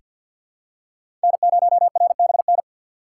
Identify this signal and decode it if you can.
Morse code, heard as a keyed tone, about 48 wpm, 700 Hz. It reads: N0RBN
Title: USA Call Signs - 5 Characters - 4x Character Spacing - 50wpm